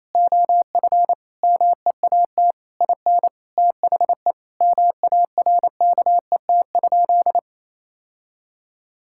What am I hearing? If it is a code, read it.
Morse code, heard as a keyed tone, about 28 wpm, 700 Hz. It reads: OF MEAT IN THE MARKET?
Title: Sentences from Top 1000 Words - Review Part 6 - 28wpm